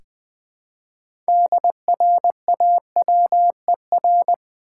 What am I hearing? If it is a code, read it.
Morse code, heard as a keyed tone, about 20 wpm, 700 Hz. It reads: DRAWER
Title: Six Letter Words - 20wpm